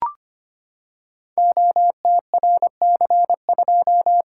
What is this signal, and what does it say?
Morse code, heard as a keyed tone, about 25 wpm, 700 Hz. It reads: OTRC2
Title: Code Groups - 5 Letter:Numbers - 25wpm